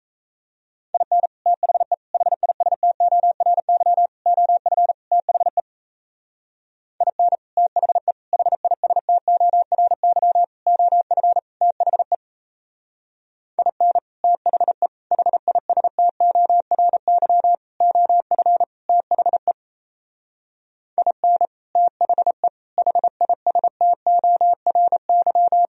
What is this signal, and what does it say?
Morse code, heard as a keyed tone, about 32 wpm, 700 Hz. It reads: IN THE HISTORY OF THE IN THE HISTORY OF THE IN THE HISTORY OF THE IN THE HISTORY
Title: Speed-Racing - Sets of 5 Words 28wpm